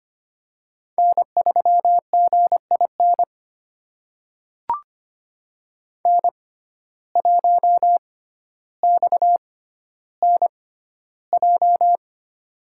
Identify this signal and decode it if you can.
Morse code, heard as a keyed tone, about 25 wpm, spaced as for 13 wpm, 700 Hz. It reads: N3GIN N1XNJ
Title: USA Call Signs - 5 Characters - 6x Character Spacing - 25wpm